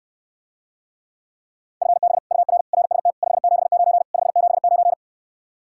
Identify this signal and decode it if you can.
Morse code, heard as a keyed tone, about 68 wpm, 700 Hz. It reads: FB UR RST 579 579